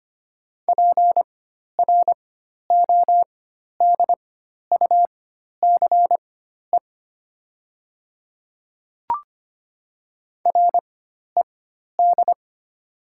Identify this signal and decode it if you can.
Morse code, heard as a keyed tone, about 25 wpm, spaced as for 12 wpm, 700 Hz. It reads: PRODUCE RED
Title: 4x Character Spacing - Top 500 Words - 25wpm